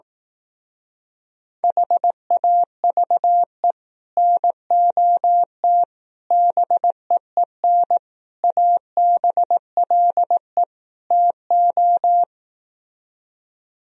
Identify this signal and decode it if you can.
Morse code, heard as a keyed tone, about 18 wpm, 700 Hz. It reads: HAVE NOT BEEN ABLE TO
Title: Speed-Racing - Sets of 5 Words 15wpm